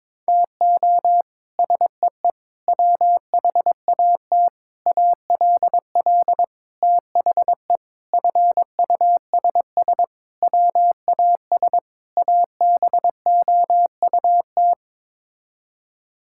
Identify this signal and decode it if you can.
Morse code, heard as a keyed tone, about 21 wpm, 700 Hz. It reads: TO SEE WHAT ALL THE FUSS WAS ABOUT
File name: practice / 22wpm / Sets of 8 Words - 22wpm